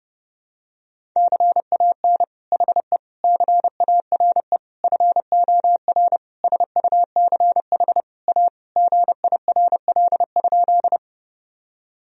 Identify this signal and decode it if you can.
Morse code, heard as a keyed tone, about 30 wpm, 700 Hz. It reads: CAN HE CARE FOR SUCH A GIRL?
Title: Sentences from Top 1000 Words - Review Part 1 - 30wpm